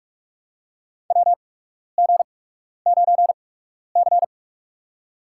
Morse code, heard as a keyed tone, about 46 wpm, spaced as for 12 wpm, 700 Hz. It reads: WG9C
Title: USA Call Signs - 4 Characters - 8x Character Spacing - 45wpm